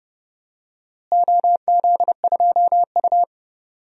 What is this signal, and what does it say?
Morse code, heard as a keyed tone, about 30 wpm, 700 Hz. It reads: OZ2U
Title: DXCC-100 Call Signs - 4 Characters - Repeated 2x - 30wpm